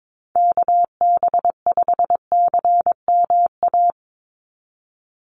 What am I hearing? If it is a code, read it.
Morse code, heard as a keyed tone, about 22 wpm, 700 Hz. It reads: KB5CMA
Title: USA Call Signs 22wpm